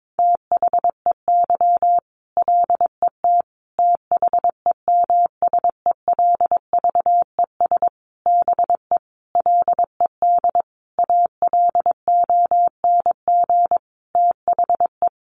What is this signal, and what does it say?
Morse code, heard as a keyed tone, about 22 wpm, 700 Hz. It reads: THEY LET THEMSELVES BE LED ALONG THE
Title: A Sound Of Thunder 22wpm